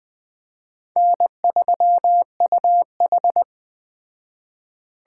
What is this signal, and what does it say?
Morse code, heard as a keyed tone, about 21 wpm, 700 Hz. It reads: N3UH
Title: Speed Racing - USA 1x2 Call Signs 17wpm